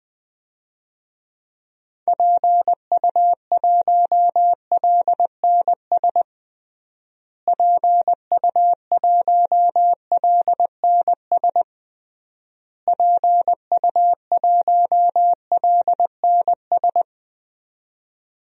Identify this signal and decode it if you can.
Morse code, heard as a keyed tone, about 19 wpm, 700 Hz. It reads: PU1LNS PU1LNS PU1LNS
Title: DXCC-100 Call Signs - 6 Characters - Repeated 3x - 20wpm